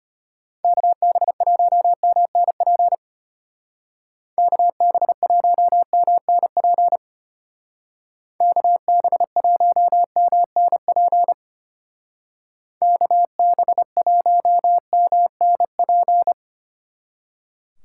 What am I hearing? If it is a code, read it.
Morse code, heard as a keyed tone, about 30 wpm, 700 Hz. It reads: KB1MNP KB1MNP KB1MNP KB1MNP
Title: Speed Racing - USA 2x3 Call Signs 25wpm